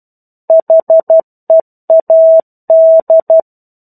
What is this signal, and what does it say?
Morse code, heard as a keyed tone, about 12 wpm, 650 Hz. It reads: HEAD
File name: 01-Chapter-01 12 WPM